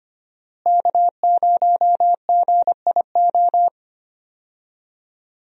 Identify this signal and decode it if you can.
Morse code, heard as a keyed tone, about 25 wpm, 700 Hz. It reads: K0GIO
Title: USA Call Signs - 5 Characters - Repeated 3x - 25wpm